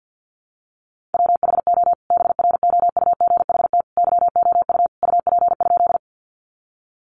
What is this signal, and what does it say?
Morse code, heard as a keyed tone, about 50 wpm, 700 Hz. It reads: WHO BROUGHT YOU UP?
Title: Sentences from Top 500 Words 50wpm